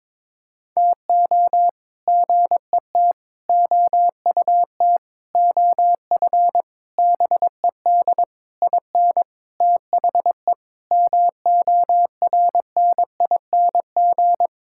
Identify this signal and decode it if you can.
Morse code, heard as a keyed tone, about 22 wpm, 700 Hz. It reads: TO GET OUT OF BED IN THE MORNING
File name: practice / 22wpm / Sets of 8 Words - 22wpm